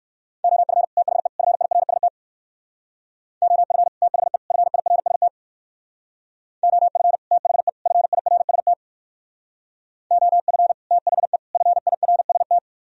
Morse code, heard as a keyed tone, about 57 wpm, 700 Hz. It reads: OF THE FIRST OF THE FIRST OF THE FIRST OF THE FIRST